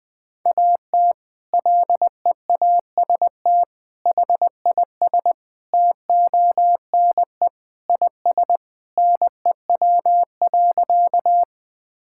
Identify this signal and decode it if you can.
Morse code, heard as a keyed tone, about 20 wpm, 700 Hz. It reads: AT LEAST HIS TONE IS NEW.
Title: Sentences from Top 1600 Words 20wpm